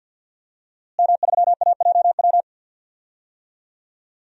Morse code, heard as a keyed tone, about 52 wpm, 700 Hz. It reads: M3AJW
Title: DXCC-100 Call Signs - 5 Characters - Repeated 2x - 50wpm